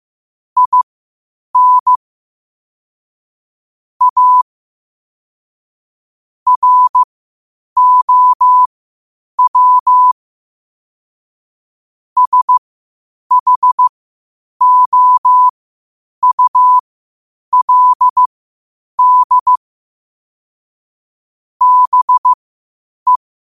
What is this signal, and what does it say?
Morse code, heard as a keyed tone, about 15 wpm, spaced as for 8 wpm, 1000 Hz. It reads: IN A ROW SHOULD BE
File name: qotdfiles / 8wpm